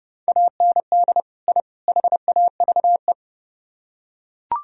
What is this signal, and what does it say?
Morse code, heard as a keyed tone, about 30 wpm, 700 Hz. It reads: AND I HAVE